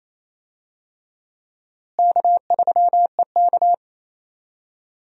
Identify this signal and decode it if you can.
Morse code, heard as a keyed tone, about 28 wpm, 700 Hz. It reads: K3EK
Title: USA Call Signs - 4 Characters - Repeated 3x - 28wpm